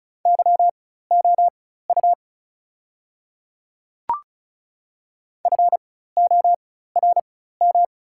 Morse code, heard as a keyed tone, about 36 wpm, spaced as for 16 wpm, 700 Hz. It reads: YOU FORM
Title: 4x Character Spacing - Top 500 Words - 35wpm